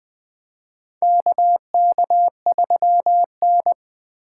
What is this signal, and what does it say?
Morse code, heard as a keyed tone, about 20 wpm, 700 Hz. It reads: KK3N
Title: USA Call Signs - 4 Characters - 2x Character Spacing - 20wpm